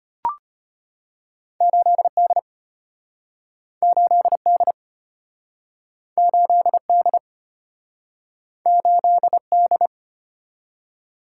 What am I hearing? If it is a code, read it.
Morse code, heard as a keyed tone, about 33 wpm, 700 Hz. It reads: 8D 8D 8D 8D